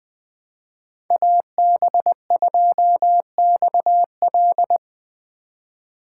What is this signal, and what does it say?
Morse code, heard as a keyed tone, about 21 wpm, 700 Hz. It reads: AB2XL
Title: Speed Racing - USA 2x2 Call Signs 17wpm